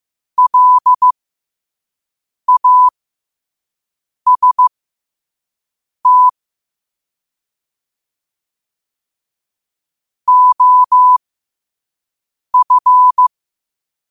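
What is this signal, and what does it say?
Morse code, heard as a keyed tone, about 15 wpm, spaced as for 5 wpm, 1000 Hz. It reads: LAST OF
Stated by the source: Morse code quote of the day sent at 5 wpm.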